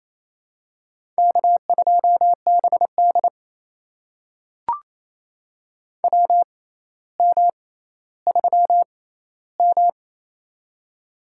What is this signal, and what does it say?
Morse code, heard as a keyed tone, about 27 wpm, spaced as for 14 wpm, 700 Hz. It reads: K2BD WM3M
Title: USA Call Signs - 4 Characters - 6x Character Spacing - 28wpm